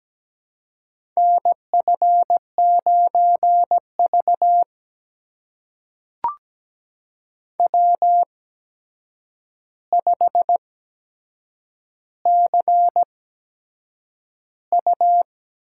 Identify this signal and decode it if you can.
Morse code, heard as a keyed tone, about 17 wpm, spaced as for 7 wpm, 700 Hz. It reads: NF9V W5CU
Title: USA Call Signs - 4 Characters - 8x Character Spacing - 17wpm